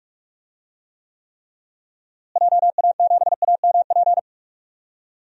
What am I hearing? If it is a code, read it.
Morse code, heard as a keyed tone, about 46 wpm, 700 Hz. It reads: JA7AMP
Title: DXCC-100 Call Signs - 6 Characters - Repeated 3x - 45wpm